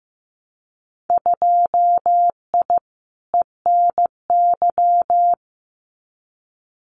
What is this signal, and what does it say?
Morse code, heard as a keyed tone, about 15 wpm, 700 Hz. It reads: 2I ENY
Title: Winter Field Day Exchanges Encore 15wpm